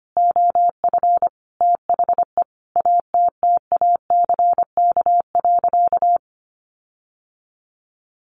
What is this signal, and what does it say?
Morse code, heard as a keyed tone, about 25 wpm, 700 Hz. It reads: OF THE ATTACK.
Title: Sentences from Top 1500 Words 25wpm